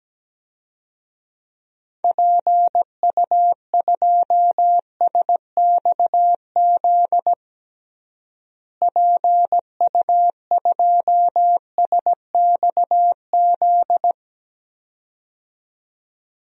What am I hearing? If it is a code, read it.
Morse code, heard as a keyed tone, about 17 wpm, 700 Hz. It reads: PU2SXZ PU2SXZ
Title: DXCC-100 Call Signs - 6 Characters - Repeated 2x - 17wpm